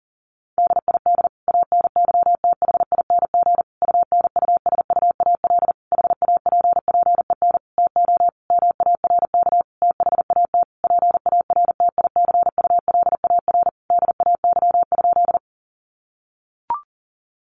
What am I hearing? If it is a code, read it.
Morse code, heard as a keyed tone, about 40 wpm, 700 Hz. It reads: DID ANYTHING UNUSUAL HAPPEN TO MARK THAT PARTICULAR DAY?
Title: Sentences from Top 1400 Words 40wpm